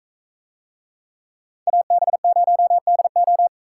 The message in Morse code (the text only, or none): AB0DO